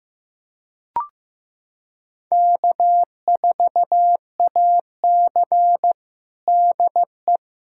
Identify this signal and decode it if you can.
Morse code, heard as a keyed tone, about 15 wpm, 700 Hz. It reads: K4AC DE
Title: USA Callsign DE Callsign - 4 Characters - 15wpm